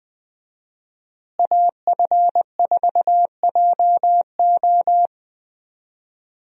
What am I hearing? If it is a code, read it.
Morse code, heard as a keyed tone, about 20 wpm, 700 Hz. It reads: AF4JO
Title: USA Call Signs - 5 Characters - Repeated 3x - 20wpm